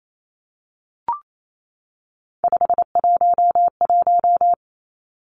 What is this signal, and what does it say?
Morse code, heard as a keyed tone, about 28 wpm, 700 Hz. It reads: 511
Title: Numbers - 3-Digit 28wpm